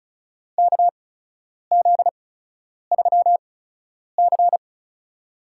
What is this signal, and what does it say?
Morse code, heard as a keyed tone, about 36 wpm, spaced as for 10 wpm, 700 Hz. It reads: KZ3C
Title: USA Call Signs - 4 Characters - 8x Character Spacing - 35wpm